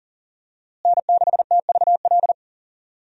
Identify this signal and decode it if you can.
Morse code, heard as a keyed tone, about 37 wpm, 700 Hz. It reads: N6TVL
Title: USA Call Signs - 5 Characters - Repeated 2x - 40wpm